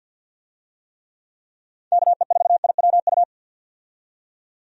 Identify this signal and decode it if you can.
Morse code, heard as a keyed tone, about 52 wpm, 700 Hz. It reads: KE4IWU
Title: USA Call Signs - 6 Characters - 50wpm